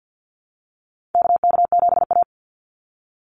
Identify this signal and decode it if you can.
Morse code, heard as a keyed tone, about 50 wpm, 700 Hz. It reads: KK7A
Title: Call Signs Repeated 2x - 50wpm